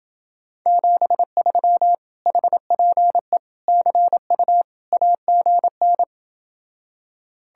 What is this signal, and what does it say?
Morse code, heard as a keyed tone, about 29 wpm, 700 Hz. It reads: 73 HPE CU AGN